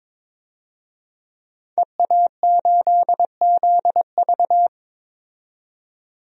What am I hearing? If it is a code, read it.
Morse code, heard as a keyed tone, about 21 wpm, 700 Hz. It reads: EA8ZV
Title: DXCC-100 Call Signs - 5 Characters - 22wpm